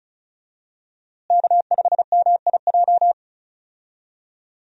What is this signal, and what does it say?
Morse code, heard as a keyed tone, about 34 wpm, 700 Hz. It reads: K5MIJ
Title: USA Call Signs - 5 Characters - Repeated 3x - 35wpm